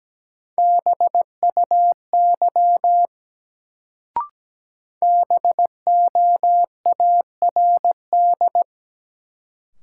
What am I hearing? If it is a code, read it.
Morse code, heard as a keyed tone, about 17 wpm, 700 Hz. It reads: BUY BOARD
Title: Sentences from Top 700 Words - New Word Drill - 17wpm